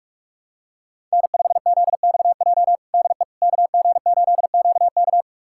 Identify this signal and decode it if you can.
Morse code, heard as a keyed tone, about 46 wpm, 700 Hz. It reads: N5ZXJ DE KG8QK
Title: USA Callsign DE Callsign - 5 Characters - 45wpm